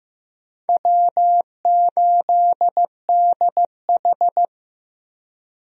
Morse code, heard as a keyed tone, about 15 wpm, 700 Hz. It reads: W8DH
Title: USA Call Signs - 4 Characters - Repeated 2x - 15wpm